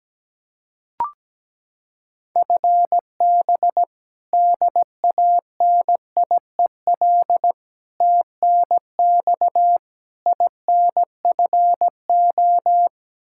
Morse code, heard as a keyed tone, about 17 wpm, 700 Hz. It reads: FB DANIEL TNX INFO
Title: Practice QSO - Exchange 3 - More Info - 17wpm